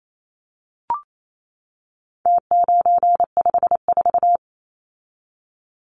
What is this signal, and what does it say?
Morse code, heard as a keyed tone, about 28 wpm, 700 Hz. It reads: T954
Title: Numbers - Contest Style - 4-Digit 28wpm